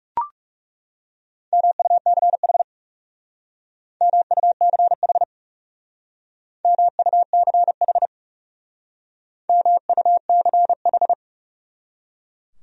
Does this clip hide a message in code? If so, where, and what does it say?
Yes: Morse code from 1.5 to 11.1 s, heard as a keyed tone, about 40 wpm, 700 Hz. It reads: MUCH MUCH MUCH MUCH